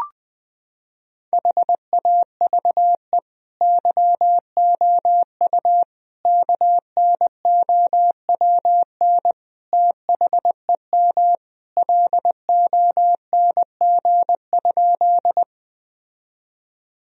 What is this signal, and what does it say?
Morse code, heard as a keyed tone, about 21 wpm, 700 Hz. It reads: HAVE YOU KNOWN THEM LONG?
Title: Sentences from Top 1000 Words - Review Part 3 - 20wpm